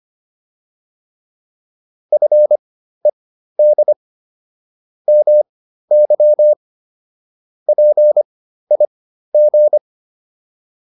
Morse code, heard as a keyed tone, about 25 wpm, spaced as for 13 wpm, 600 Hz. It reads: FED MY PIG